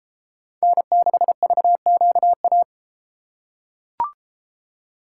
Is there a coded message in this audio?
Yes: Morse code from 0.6 to 2.6 s, heard as a keyed tone, about 32 wpm, 700 Hz. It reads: N6VQA